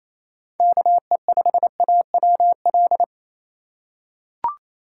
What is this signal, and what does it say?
Morse code, heard as a keyed tone, about 28 wpm, 700 Hz. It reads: KE5AWL
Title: USA Call Signs - 6 Characters - 4x Character Spacing - 28wpm